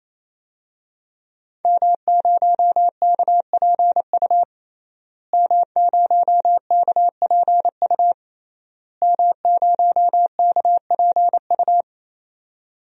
Morse code, heard as a keyed tone, about 28 wpm, 700 Hz. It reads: M0KPU M0KPU M0KPU
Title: DXCC-100 Call Signs - 5 Characters - Repeated 3x - 28wpm